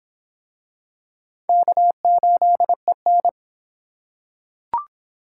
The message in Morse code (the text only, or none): K8EN